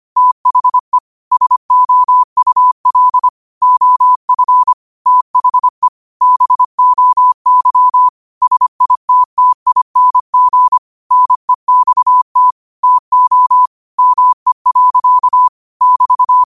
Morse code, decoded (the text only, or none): THE SOUL OF THE BOY SITTING NEXT TO ME. <BT>